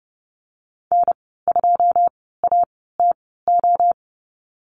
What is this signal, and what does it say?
Morse code, heard as a keyed tone, about 30 wpm, spaced as for 17 wpm, 700 Hz. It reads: N2ATO
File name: Lesson 14g - Number 2 - Call Signs 30wpm